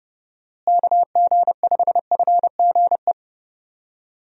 Morse code, heard as a keyed tone, about 30 wpm, 700 Hz. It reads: KG5FGE
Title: USA Call Signs - 6 Characters - Repeated 2x - 30wpm